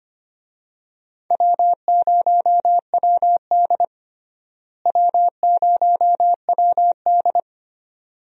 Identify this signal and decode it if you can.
Morse code, heard as a keyed tone, about 25 wpm, 700 Hz. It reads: W0WD W0WD
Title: USA Call Signs - 4 Characters - Repeated 2x - 25wpm